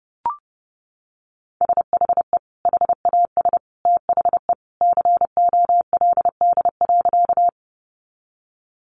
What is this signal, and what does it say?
Morse code, heard as a keyed tone, about 30 wpm, 700 Hz. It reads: SHE HAS THE COLD.
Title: Easy Phrases 30wpm